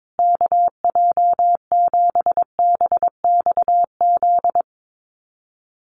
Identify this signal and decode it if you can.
Morse code, heard as a keyed tone, about 22 wpm, 700 Hz. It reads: KJ7BXZ
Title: USA Call Signs Repeated 3x 22wpm